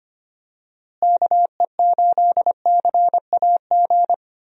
Morse code, heard as a keyed tone, about 25 wpm, 700 Hz. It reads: KE8CAG
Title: USA Call Signs - 6 Characters - Repeated 2x - 25wpm